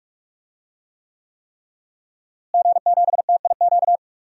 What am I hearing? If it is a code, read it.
Morse code, heard as a keyed tone, about 46 wpm, 700 Hz. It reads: G7TIQ
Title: DXCC-100 Call Signs - 5 Characters - Repeated 3x - 45wpm